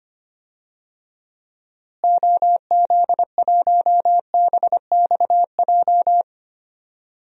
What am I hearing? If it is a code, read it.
Morse code, heard as a keyed tone, about 25 wpm, 700 Hz. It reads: OZ1BXJ
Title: DXCC-100 Call Signs - 6 Characters - Repeated 3x - 25wpm